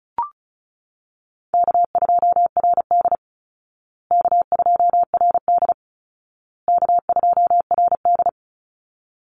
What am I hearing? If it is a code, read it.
Morse code, heard as a keyed tone, about 34 wpm, 700 Hz. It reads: K2RD K2RD K2RD
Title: Easy Call Signs Repeated 3x - 35wpm